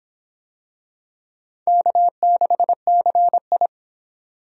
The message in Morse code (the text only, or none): K6CI